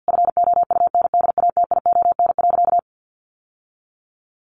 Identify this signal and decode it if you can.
Morse code, heard as a keyed tone, about 50 wpm, 700 Hz. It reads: FOUNDATION.
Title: Sentences from Top 1200 Words 50wpm